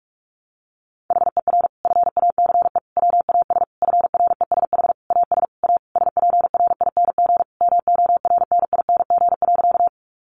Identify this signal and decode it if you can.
Morse code, heard as a keyed tone, about 44 wpm, 700 Hz. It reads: HER FACE WAS FRESH AS A SPRING MORNING.
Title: Sentences from Top 1200 Words 45wpm